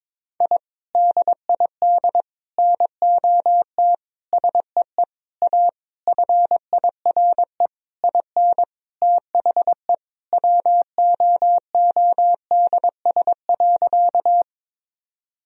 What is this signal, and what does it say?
Morse code, heard as a keyed tone, about 22 wpm, 700 Hz. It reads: I DID NOT SEE A FIRE IN THE WOODS.
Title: Sentences from Top 1000 Words - Review Part 2 - 22wpm